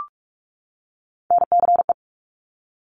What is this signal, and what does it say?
Morse code, heard as a keyed tone, about 46 wpm, 700 Hz. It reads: NCE
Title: Top Three Letter Combinations in Words 45wpm